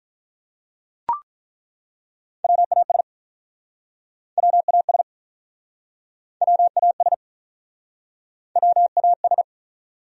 Morse code, heard as a keyed tone, about 50 wpm, 700 Hz. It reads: WAS WAS WAS WAS